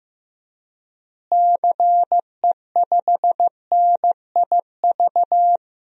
Morse code, heard as a keyed tone, about 16 wpm, 700 Hz. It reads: CE5NIV
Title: DXCC-100 Call Signs - 6 Characters - Repeated 3x - 15wpm